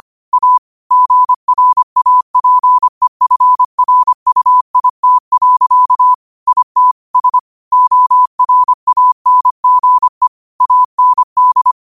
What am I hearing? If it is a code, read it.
Morse code, heard as a keyed tone, about 25 wpm, 1000 Hz. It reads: A GRAPEFRUIT. IT S ORANGE AND
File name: qotdfiles / 25wpm